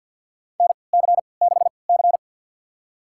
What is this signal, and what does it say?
Morse code, heard as a keyed tone, about 47 wpm, spaced as for 28 wpm, 700 Hz. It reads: NC6C
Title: USA Call Signs - 4 Characters - 3x Character Spacing - 50wpm